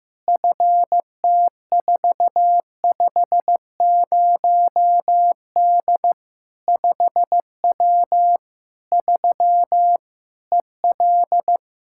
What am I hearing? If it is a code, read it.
Morse code, heard as a keyed tone, about 15 wpm, 700 Hz. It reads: FT450D 5W 3 EL
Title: Practice QSO - Exchange 2 - Info - 15wpm